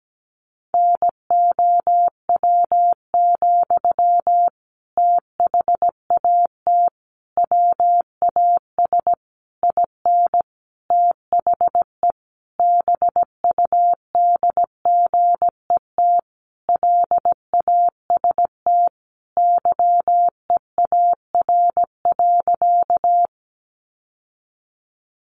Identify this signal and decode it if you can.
Morse code, heard as a keyed tone, about 17 wpm, 700 Hz. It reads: NOW, THAT WAS IN THE BUDGET LAST YEAR.
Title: Sentences from Top 1200 Words 17wpm